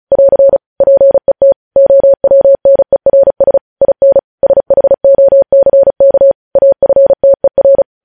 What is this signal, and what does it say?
Morse code, heard as a keyed tone, about 36 wpm, 550 Hz. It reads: <AR> PET OWNERS IN SHOCK AFTER
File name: podcast / News headlines-1x-35x35